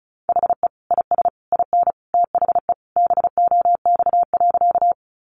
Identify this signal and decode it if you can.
Morse code, heard as a keyed tone, about 34 wpm, 700 Hz. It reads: HE IS IN THE BOX.
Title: Sentences from Top 500 Words 35wpm